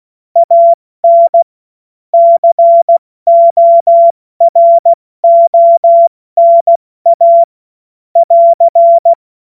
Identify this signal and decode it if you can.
Morse code, heard as a keyed tone, about 16 wpm, 680 Hz. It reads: AN CORONA <AR>